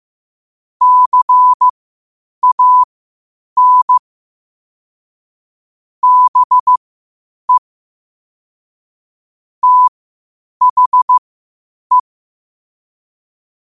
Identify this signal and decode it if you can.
Morse code, heard as a keyed tone, about 15 wpm, spaced as for 8 wpm, 1000 Hz. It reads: CAN BE THE
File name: qotdfiles / 8wpm